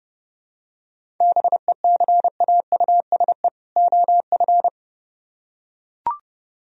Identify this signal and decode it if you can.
Morse code, heard as a keyed tone, about 30 wpm, 700 Hz. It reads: BECAUSE OF